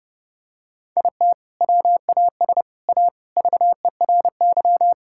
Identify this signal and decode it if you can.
Morse code, heard as a keyed tone, about 32 wpm, 700 Hz. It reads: IT WAS A VERY
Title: Sentences from Top 1000 Words - Review Part 6 - 30wpm